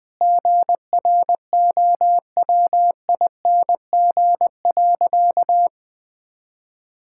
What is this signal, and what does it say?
Morse code, heard as a keyed tone, about 21 wpm, 700 Hz. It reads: GROWING.
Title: Sentences with 2-Word Phrases - Spoken Beginning - 20wpm